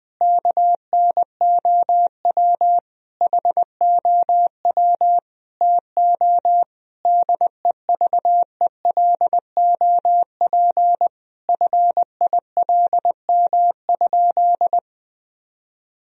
Morse code, heard as a keyed tone, about 20 wpm, 700 Hz. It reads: KNOW HOW TO DEVELOP FILM?
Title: Sentences from Top 1000 Words - Review Part 3 - 20wpm